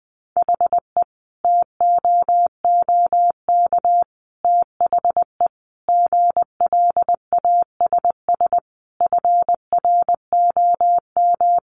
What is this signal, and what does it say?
Morse code, heard as a keyed tone, about 20 wpm, 700 Hz. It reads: HE TOOK THE GLASS FROM
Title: Sentences from Top 1300 Words 20wpm